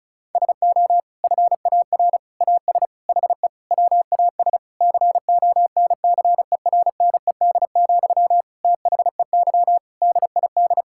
Morse code, heard as a keyed tone, about 36 wpm, 700 Hz. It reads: SO FAR AS HE WAS CONCERNED, THEY DID
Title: Sentences from Top 1000 Words - Review Part 6 - 35wpm